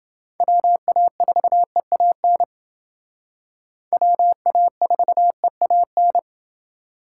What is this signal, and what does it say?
Morse code, heard as a keyed tone, about 29 wpm, 700 Hz. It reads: WA4EAN WA4EAN